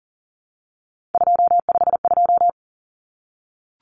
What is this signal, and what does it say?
Morse code, heard as a keyed tone, about 40 wpm, 700 Hz. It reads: 252
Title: Common Cut Numbers - 3-Digit 40wpm